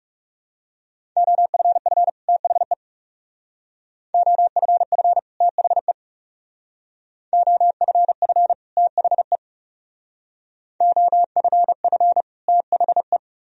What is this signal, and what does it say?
Morse code, heard as a keyed tone, about 35 wpm, 700 Hz. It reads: OFF THE OFF THE OFF THE OFF THE
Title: Speed-Racing - Sets of 2 Words 30wpm